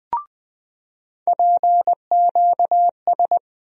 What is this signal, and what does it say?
Morse code, heard as a keyed tone, about 20 wpm, 700 Hz. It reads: PQS
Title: USA Call Sign Postfixes - 3 Letters 20wpm